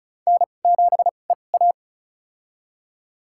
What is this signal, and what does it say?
Morse code, heard as a keyed tone, about 36 wpm, spaced as for 25 wpm, 700 Hz. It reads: N7EA
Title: USA Call Signs - 4 Characters - 2x Character Spacing - 35wpm